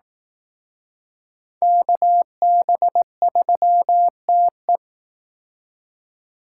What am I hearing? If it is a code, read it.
Morse code, heard as a keyed tone, about 18 wpm, 700 Hz. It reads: KB3TE
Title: Speed Racing - USA 2x2 Call Signs 15wpm